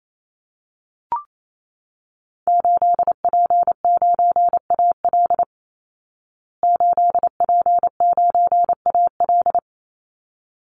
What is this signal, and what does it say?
Morse code, heard as a keyed tone, about 28 wpm, 700 Hz. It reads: 8P9AL 8P9AL
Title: Call Signs Repeated 3x - 28wpm